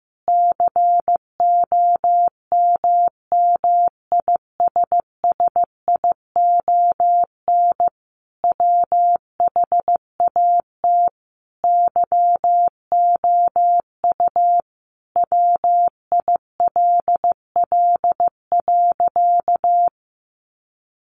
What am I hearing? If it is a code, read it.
Morse code, heard as a keyed tone, about 15 wpm, 700 Hz. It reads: COMMISSION WHAT YOU WILL.
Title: Sentences from Top 1400 Words 15wpm